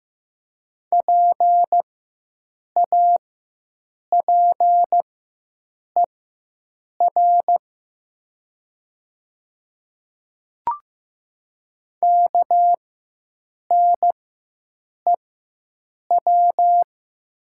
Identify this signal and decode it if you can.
Morse code, heard as a keyed tone, about 15 wpm, spaced as for 7 wpm, 700 Hz. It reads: PAPER KNEW
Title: 4x Character Spacing - Top 500 Words - 15wpm